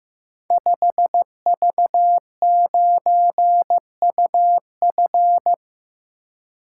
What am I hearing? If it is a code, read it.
Morse code, heard as a keyed tone, about 15 wpm, 700 Hz. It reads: 5V9UF
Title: Code Groups - 5 Letter:Numbers - 15wpm